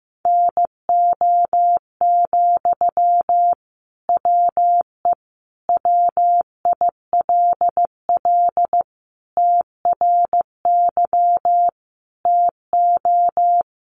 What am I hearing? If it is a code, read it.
Morse code, heard as a keyed tone, about 15 wpm, 700 Hz. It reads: NO, WE WILL TRY TO
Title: Sentences from Top 1100 Words 15wpm